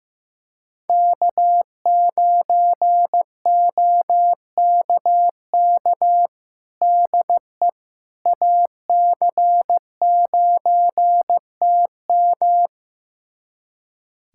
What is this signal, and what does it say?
Morse code, heard as a keyed tone, about 16 wpm, 700 Hz. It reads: K9OKK DE AC9TM
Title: USA Callsign DE Callsign - 5 Characters - 15wpm